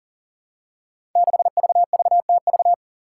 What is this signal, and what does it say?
Morse code, heard as a keyed tone, about 40 wpm, 700 Hz. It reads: 6VVTV
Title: Five Challenging Letter-Number Combinations 40wpm